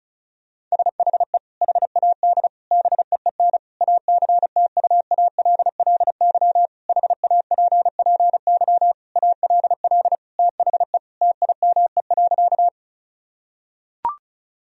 Morse code, heard as a keyed tone, about 36 wpm, 700 Hz. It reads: SHE HAD BEEN ACTUALLY HAPPY ALL THE TIME.
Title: Sentences from Top 1000 Words - Review Part 4 - 35wpm